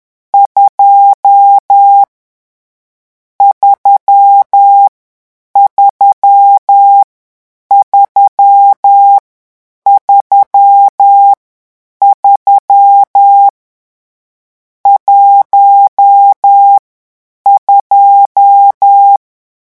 Morse code, heard as a keyed tone, about 10 wpm, 800 Hz. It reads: 2 3 3 3 3 3 1 2
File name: morse mp3 / lecon10-vitesse 10